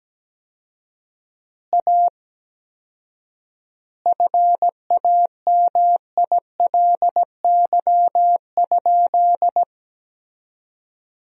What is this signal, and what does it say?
Morse code, heard as a keyed tone, about 17 wpm, 700 Hz. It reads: A FAMILY?
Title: 4x Word-Spacing - Sentences from Top 400 Words - 17wpm